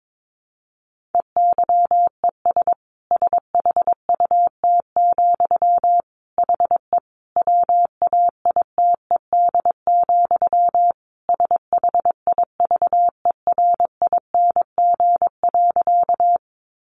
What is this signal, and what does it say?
Morse code, heard as a keyed tone, about 22 wpm, 700 Hz. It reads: EYES SHUT, HE WAITED, SHIVERING.
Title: A Sound Of Thunder 22wpm